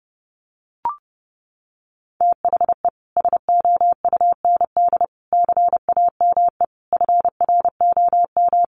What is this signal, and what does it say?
Morse code, heard as a keyed tone, about 30 wpm, 700 Hz. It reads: THE SOUND CAME FROM
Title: Sentences from Top 1400 Words 30wpm